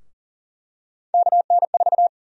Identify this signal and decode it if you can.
Morse code, heard as a keyed tone, about 40 wpm, 700 Hz. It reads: KN4
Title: USA Call Sign Prefixes - Repeated 2x - 40wpm